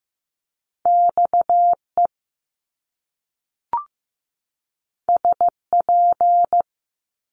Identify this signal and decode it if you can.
Morse code, heard as a keyed tone, about 15 wpm, 700 Hz. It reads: XE SP
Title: Top Two Letter Combinations in Words 15wpm